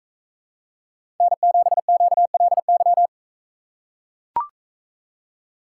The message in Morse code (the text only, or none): N7QLY